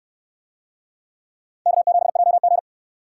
Morse code, heard as a keyed tone, about 65 wpm, 700 Hz. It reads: K7JG